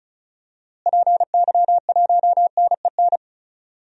Morse code, heard as a keyed tone, about 36 wpm, 700 Hz. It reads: PY1NEN
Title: DXCC-100 Call Signs - 6 Characters - Repeated 3x - 35wpm